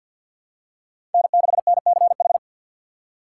Morse code, heard as a keyed tone, about 52 wpm, 700 Hz. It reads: N6NCH
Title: USA Call Signs - 5 Characters - 6x Character Spacing - 50wpm